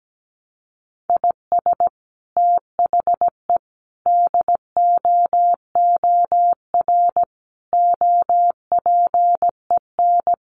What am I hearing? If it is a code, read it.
Morse code, heard as a keyed tone, about 17 wpm, 700 Hz. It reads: IS THE DOOR OPEN
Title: Sentences from Top 1300 Words 17wpm